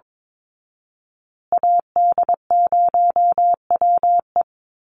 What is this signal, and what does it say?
Morse code, heard as a keyed tone, about 22 wpm, 700 Hz. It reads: AD0WE
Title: Call Signs Repeated 2x - 22wpm